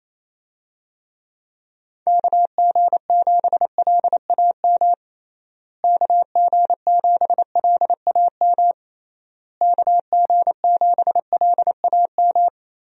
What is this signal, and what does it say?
Morse code, heard as a keyed tone, about 27 wpm, 700 Hz. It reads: KG7LAM KG7LAM KG7LAM
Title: USA Call Signs - 6 Characters - Repeated 3x - 28wpm